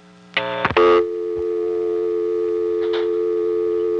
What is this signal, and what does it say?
Dial tone (North America), 350 + 440 Hz.